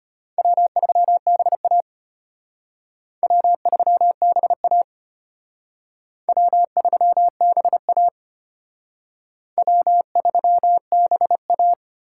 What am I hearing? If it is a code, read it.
Morse code, heard as a keyed tone, about 32 wpm, 700 Hz. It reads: W3BA W3BA W3BA W3BA